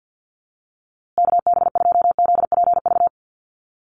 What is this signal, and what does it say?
Morse code, heard as a keyed tone, about 50 wpm, 700 Hz. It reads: KB2ZPV
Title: USA Call Signs Repeated 2x 50wpm